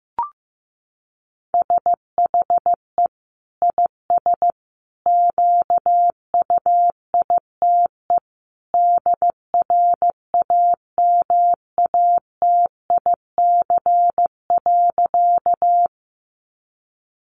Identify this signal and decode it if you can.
Morse code, heard as a keyed tone, about 15 wpm, 700 Hz. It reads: SHE IS QUITE DRAMATIC.
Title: Sentences from Top 1300 Words 15wpm